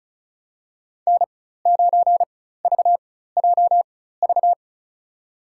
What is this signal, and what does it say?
Morse code, heard as a keyed tone, about 34 wpm, spaced as for 16 wpm, 700 Hz. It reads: N9VJV
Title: USA Call Signs - 5 Characters - 4x Character Spacing - 35wpm